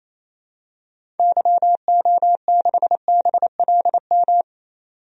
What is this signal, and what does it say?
Morse code, heard as a keyed tone, about 28 wpm, 700 Hz. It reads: YO6BLM
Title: DXCC-100 Call Signs - 6 Characters - 28wpm